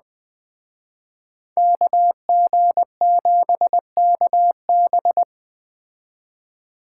Morse code, heard as a keyed tone, about 20 wpm, 700 Hz. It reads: KG7KB
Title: Speed Racing - USA 2x2 Call Signs 17wpm